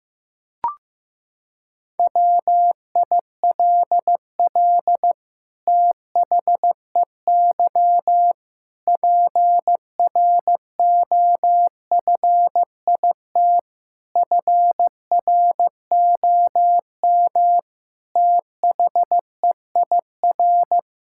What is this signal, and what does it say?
Morse code, heard as a keyed tone, about 15 wpm, 700 Hz. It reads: WILL THEY PROFIT FROM THEIR
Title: Sentences from Top 1000 Words - Review Part 6 - 15wpm